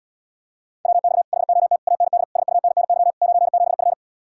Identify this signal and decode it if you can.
Morse code, heard as a keyed tone, about 79 wpm, 700 Hz. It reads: WX HOT AND SUNNY 97F